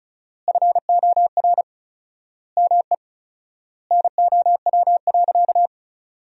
Morse code, heard as a keyed tone, about 36 wpm, 700 Hz. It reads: FOR ME NOW.
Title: 4x Word-Spacing - Sentences from Top 200 Words - 35wpm